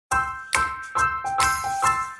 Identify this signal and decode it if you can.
There is no code.